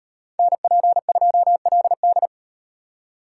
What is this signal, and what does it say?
Morse code, heard as a keyed tone, about 35 wpm, 700 Hz. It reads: NP2LD